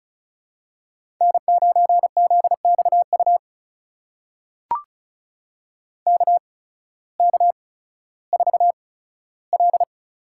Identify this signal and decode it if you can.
Morse code, heard as a keyed tone, about 36 wpm, spaced as for 16 wpm, 700 Hz. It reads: N9ZXU KK4L
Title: USA Call Signs - 8x Character Spacing - 35wpm